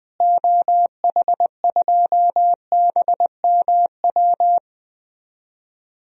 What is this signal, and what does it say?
Morse code, heard as a keyed tone, about 20 wpm, 700 Hz. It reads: OH2BMW
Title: DXCC-100 Call Signs - 6 Characters - Repeated 2x - 20wpm